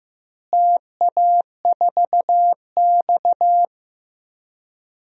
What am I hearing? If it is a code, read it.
Morse code, heard as a keyed tone, about 15 wpm, 700 Hz. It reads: TA4X
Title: DXCC-100 Call Signs - 4 Characters - Repeated 3x - 15wpm